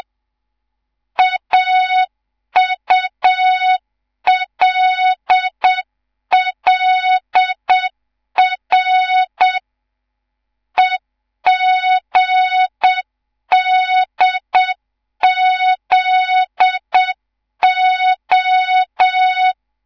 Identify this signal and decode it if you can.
Morse code, heard as a keyed tone, about 7 wpm, 3700 Hz. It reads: AULLR EGDZO
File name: Morze-HA2MN / hang / 11 GYAKORLAT